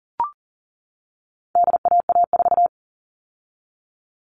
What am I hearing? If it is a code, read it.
Morse code, heard as a keyed tone, about 40 wpm, 700 Hz. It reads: DAA4
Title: All Cut Numbers - 4-Digit 40wpm